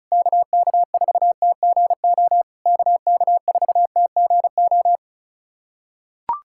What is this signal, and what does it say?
Morse code, heard as a keyed tone, about 36 wpm, 690 Hz. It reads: KK4TGO KK4TGO
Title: SOTA QSO Practice 35wpm